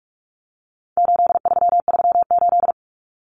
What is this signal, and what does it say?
Morse code, heard as a keyed tone, about 44 wpm, 700 Hz. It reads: 8338
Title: Numbers - 4-Digit 45wpm